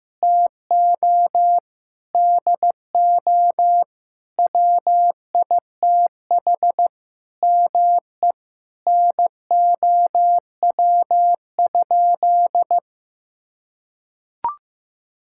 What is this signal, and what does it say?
Morse code, heard as a keyed tone, about 16 wpm, 700 Hz. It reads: TO DO WITH ME NOW?
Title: Sentences from Top 1600 Words 15wpm